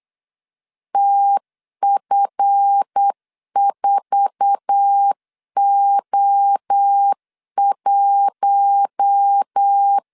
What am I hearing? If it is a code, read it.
Morse code, heard as a keyed tone, about 8 wpm, 800 Hz. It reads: TF4O1
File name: LEZIONI / 8 YJ45 / 8 YJ4 8wpm 3246